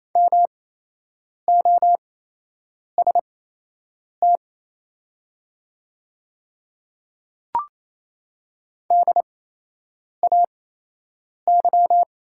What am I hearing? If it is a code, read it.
Morse code, heard as a keyed tone, about 28 wpm, spaced as for 8 wpm, 700 Hz. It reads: MOST DAY
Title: 8x Character Spacing - Top 100 Words - 28wpm